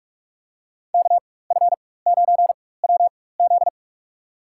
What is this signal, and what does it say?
Morse code, heard as a keyed tone, about 46 wpm, spaced as for 21 wpm, 700 Hz. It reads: KF9WZ
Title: USA Call Signs - 5 Characters - 4x Character Spacing - 45wpm